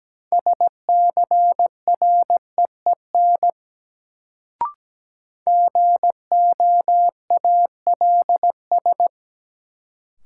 Morse code, heard as a keyed tone, about 17 wpm, 700 Hz. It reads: SCREEN GOALS
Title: Sentences from Top 1000 Words - Review Part 6 - New Word Drill - 17wpm